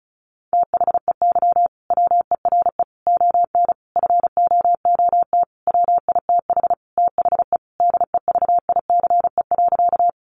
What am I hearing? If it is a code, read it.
Morse code, heard as a keyed tone, about 34 wpm, 700 Hz. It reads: THEY WERE ON FOOT WITH THE DEVICE.
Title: Sentences from Top 1500 Words 35wpm